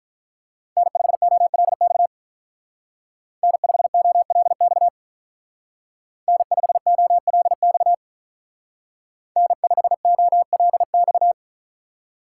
Morse code, heard as a keyed tone, about 43 wpm, 700 Hz. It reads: N5OLX N5OLX N5OLX N5OLX
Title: Speed Racing - USA 1x3 Call Signs 35wpm